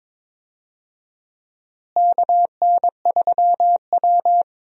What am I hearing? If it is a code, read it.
Morse code, heard as a keyed tone, about 22 wpm, 700 Hz. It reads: KN3W
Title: USA Call Signs - 4 Characters - Repeated 2x - 22wpm